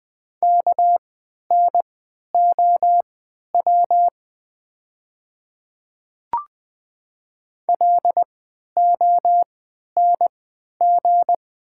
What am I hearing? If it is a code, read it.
Morse code, heard as a keyed tone, about 20 wpm, spaced as for 11 wpm, 700 Hz. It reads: KNOW LONG
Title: 3x Character Spacing - Top 100 Words - 20wpm